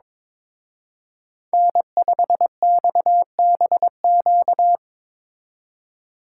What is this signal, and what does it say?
Morse code, heard as a keyed tone, about 24 wpm, 700 Hz. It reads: N5XBQ
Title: Speed Racing - USA Call Signs 22wpm